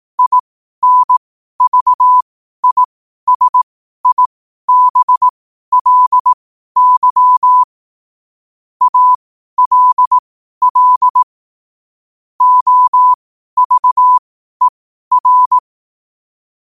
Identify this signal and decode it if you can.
Morse code, heard as a keyed tone, about 18 wpm, spaced as for 12 wpm, 1000 Hz. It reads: INVISIBLY ALL OVER